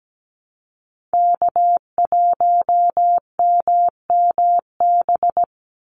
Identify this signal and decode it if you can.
Morse code, heard as a keyed tone, about 17 wpm, 700 Hz. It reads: K1MMB
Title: Call Signs - 17wpm